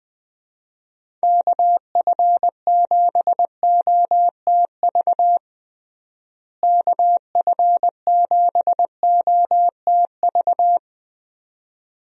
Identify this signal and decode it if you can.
Morse code, heard as a keyed tone, about 20 wpm, 700 Hz. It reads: KF7OTV KF7OTV
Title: USA Call Signs - 6 Characters - Repeated 2x - 20wpm